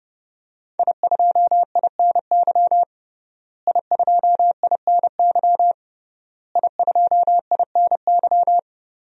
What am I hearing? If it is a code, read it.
Morse code, heard as a keyed tone, about 30 wpm, 700 Hz. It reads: I2INY I2INY I2INY
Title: DXCC-100 Call Signs - 5 Characters - Repeated 3x - 30wpm